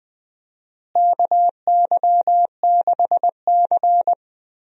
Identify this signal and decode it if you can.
Morse code, heard as a keyed tone, about 21 wpm, 700 Hz. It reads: KY6C
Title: USA Call Signs - 4 Characters - Repeated 2x - 20wpm